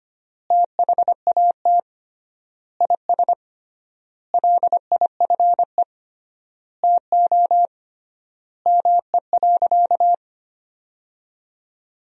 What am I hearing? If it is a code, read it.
Morse code, heard as a keyed tone, about 25 wpm, 700 Hz. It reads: THAT IS LIFE TO ME.
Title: 3x Word-Spacing - Sentences from Top 500 Words - 25wpm